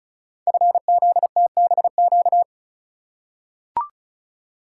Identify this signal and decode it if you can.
Morse code, heard as a keyed tone, about 36 wpm, 700 Hz. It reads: FZTBQ
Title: Code Groups - 5 Letters - 35wpm